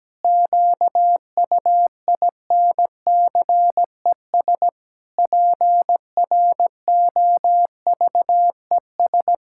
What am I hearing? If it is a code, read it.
Morse code, heard as a keyed tone, about 17 wpm, 700 Hz. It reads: QUINCES PROVES
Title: Pangram Challenge v2 - 17wpm